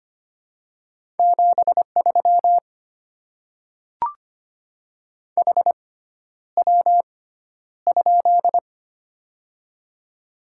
Morse code, heard as a keyed tone, about 25 wpm, spaced as for 11 wpm, 700 Hz. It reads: 73 HW?
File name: Top 20 QSO Elements - 6x Character Spacing - 25wpm